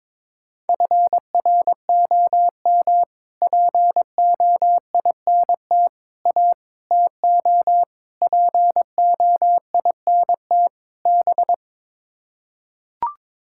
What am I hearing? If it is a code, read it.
Morse code, heard as a keyed tone, about 22 wpm, 700 Hz. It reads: FROM POINT A TO POINT B
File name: practice / 22wpm / Sets of 6 Words - 22wpm